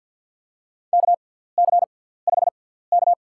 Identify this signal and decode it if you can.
Morse code, heard as a keyed tone, about 47 wpm, spaced as for 17 wpm, 700 Hz. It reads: KC5K
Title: USA Call Signs - 6 Characters - 6x Character Spacing - 50wpm